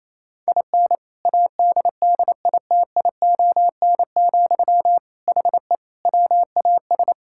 Morse code, heard as a keyed tone, about 28 wpm, 700 Hz. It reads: IN ADDITION, HE WAS
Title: Sentences from Top 1000 Words - Review Part 6 - 28wpm